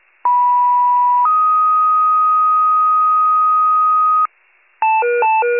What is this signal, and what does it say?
Two-tone page, heard as a one-second tone, then a three-second tone, about 980 then 1250 Hz. It is heard from 0.2 to 4.3 s.